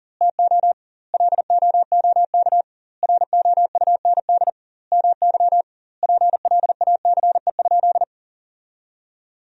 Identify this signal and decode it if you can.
Morse code, heard as a keyed tone, about 40 wpm, 700 Hz. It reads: TO LOOK ROUND MY PLACE?
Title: 2x Word-Spacing - Sentences from Top 200 Words - 40wpm